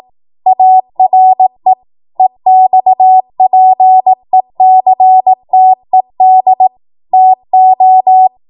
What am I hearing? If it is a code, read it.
Morse code, heard as a keyed tone, about 18 wpm, 750 Hz. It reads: ARE EXPECTED TO
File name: Morse / 18WPM